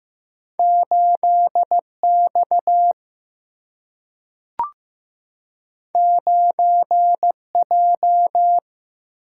Morse code, heard as a keyed tone, about 15 wpm, 700 Hz. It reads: 8 X 9 J
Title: Single-Letter Single-Number 15wpm